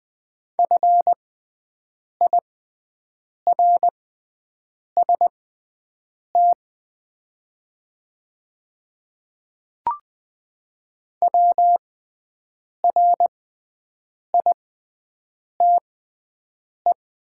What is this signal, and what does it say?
Morse code, heard as a keyed tone, about 20 wpm, spaced as for 7 wpm, 700 Hz. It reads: FIRST WRITE
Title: 6x Character Spacing - Top 100 Words - 20wpm